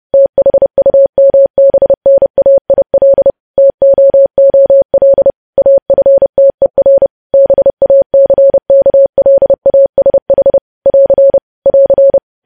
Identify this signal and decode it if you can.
Morse code, heard as a keyed tone, about 30 wpm, 550 Hz. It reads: THUMBNAIL TOOL AFTER BACKLASH <AR> <AR>
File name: podcast / News headlines-1x-30x30